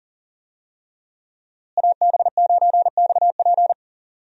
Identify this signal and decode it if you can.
Morse code, heard as a keyed tone, about 40 wpm, 700 Hz. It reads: AB9XP
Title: USA Call Signs - 5 Characters - Repeated 2x - 40wpm